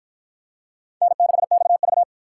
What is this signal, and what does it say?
Morse code, heard as a keyed tone, about 48 wpm, 700 Hz. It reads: N6XV